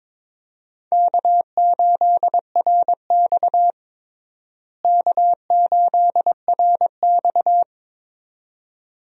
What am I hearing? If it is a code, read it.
Morse code, heard as a keyed tone, about 22 wpm, 700 Hz. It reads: K8RX K8RX
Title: USA Call Signs - 4 Characters - Repeated 2x - 22wpm